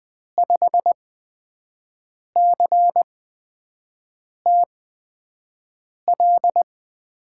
Morse code, heard as a keyed tone, about 21 wpm, spaced as for 5 wpm, 700 Hz. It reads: 5CTL
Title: USA Call Signs - 6 Characters - 8x Character Spacing - 20wpm